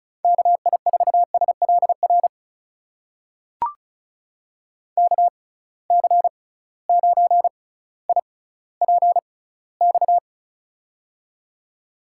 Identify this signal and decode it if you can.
Morse code, heard as a keyed tone, about 36 wpm, spaced as for 18 wpm, 700 Hz. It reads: KI4SLR KC9IPX
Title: USA Call Signs - 6 Characters - 6x Character Spacing - 35wpm